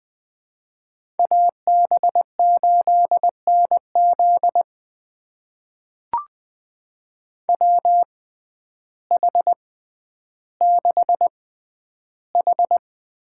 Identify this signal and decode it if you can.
Morse code, heard as a keyed tone, about 20 wpm, spaced as for 11 wpm, 700 Hz. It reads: AB8NZ WH6H
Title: USA Call Signs - 5 Characters - 6x Character Spacing - 20wpm